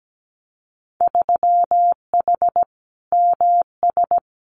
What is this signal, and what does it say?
Morse code, heard as a keyed tone, about 17 wpm, 700 Hz. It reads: 3H MS
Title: Winter Field Day Exchanges Encore 17wpm